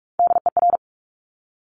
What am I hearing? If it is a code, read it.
Morse code, heard as a keyed tone, about 46 wpm, 690 Hz. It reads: DER
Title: Top Three Letter Combinations in Words 45wpm